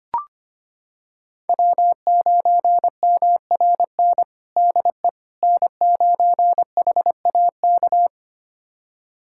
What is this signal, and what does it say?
Morse code, heard as a keyed tone, about 25 wpm, 700 Hz. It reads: W9MRN DE N9HAK
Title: USA Callsign DE Callsign - 5 Characters - 25wpm